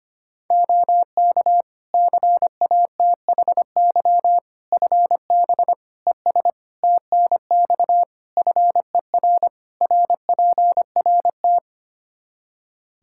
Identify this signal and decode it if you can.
Morse code, heard as a keyed tone, about 25 wpm, 700 Hz. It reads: OK CATHY FB ES TNX FER RPRT
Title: K1USN SST QSO Practice 25wpm